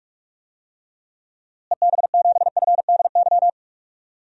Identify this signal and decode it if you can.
Morse code, heard as a keyed tone, about 46 wpm, 700 Hz. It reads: EB7FDY
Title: DXCC-100 Call Signs - 6 Characters - Repeated 3x - 45wpm